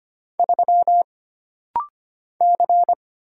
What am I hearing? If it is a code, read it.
Morse code, heard as a keyed tone, about 25 wpm, 700 Hz. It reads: 3 C
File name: Single Character (Letter:Number:Punctuation:Prosign) - Rapid-Fire - 25wpm